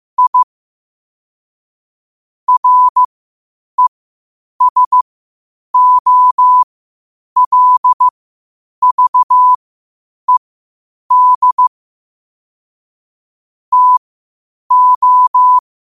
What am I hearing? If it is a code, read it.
Morse code, heard as a keyed tone, about 15 wpm, spaced as for 8 wpm, 1000 Hz. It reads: I RESOLVED TO
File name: qotdfiles / 8wpm